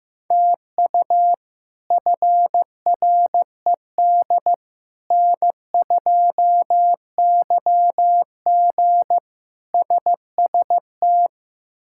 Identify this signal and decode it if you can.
Morse code, heard as a keyed tone, about 15 wpm, 700 Hz. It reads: TU FRED N2YG SST
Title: K1USN SST QSO Practice - 15wpm